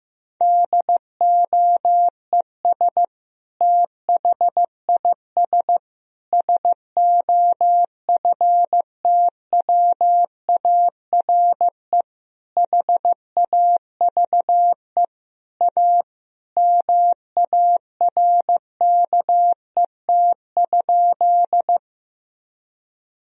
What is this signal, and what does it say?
Morse code, heard as a keyed tone, about 16 wpm, 700 Hz. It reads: DOES THIS SOFTWARE HAVE A MARKET?
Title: Sentences from Top 1000 Words - Review Part 3 - 15wpm